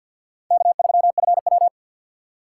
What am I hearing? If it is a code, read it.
Morse code, heard as a keyed tone, about 47 wpm, 700 Hz. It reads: K3FW
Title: USA Call Signs - 4 Characters - 2x Character Spacing - 50wpm